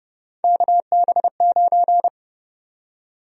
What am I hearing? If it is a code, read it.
Morse code, heard as a keyed tone, about 30 wpm, 700 Hz. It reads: KB9
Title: USA Call Sign Prefixes - 30wpm